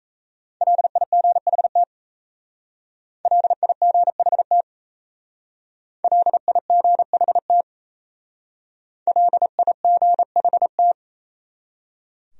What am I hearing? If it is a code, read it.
Morse code, heard as a keyed tone, about 35 wpm, 700 Hz. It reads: LIGHT LIGHT LIGHT LIGHT
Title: Speed-Racing Top 300 Words 28wpm